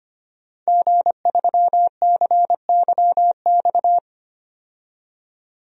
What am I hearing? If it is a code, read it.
Morse code, heard as a keyed tone, about 25 wpm, 700 Hz. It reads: G3CYX
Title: DXCC-100 Call Signs - 5 Characters - 25wpm